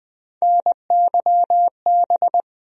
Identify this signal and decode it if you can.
Morse code, heard as a keyed tone, about 20 wpm, 700 Hz. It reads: NYB
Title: USA Call Sign Postfixes - 3 Letters 20wpm